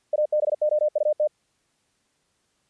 Morse code, heard as a keyed tone, about 48 wpm, 600 Hz. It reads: ABOUT
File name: archived / CRSnet